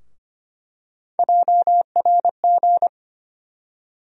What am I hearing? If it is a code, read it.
Morse code, heard as a keyed tone, about 25 wpm, 700 Hz. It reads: JRG
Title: USA Call Sign Postfixes - 3 Letters 25wpm